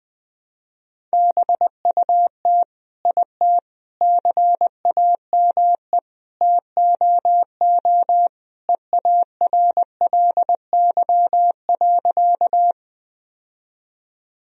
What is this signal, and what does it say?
Morse code, heard as a keyed tone, about 20 wpm, 700 Hz. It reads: BUT IT CAME TOO EARLY.
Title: Sentences from Top 1000 Words - Review Part 2 - 20wpm